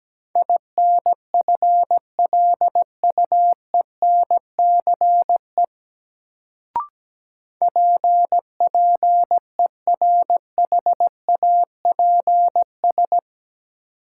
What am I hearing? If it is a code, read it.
Morse code, heard as a keyed tone, about 17 wpm, 700 Hz. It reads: INFLUENCE PPERHAPS
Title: Sentences from Top 1100 Words - New Word Drill Alt - 17wpm